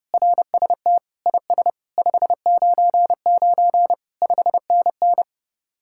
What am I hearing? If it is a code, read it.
Morse code, heard as a keyed tone, about 30 wpm, 700 Hz. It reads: RST IS 599 5NN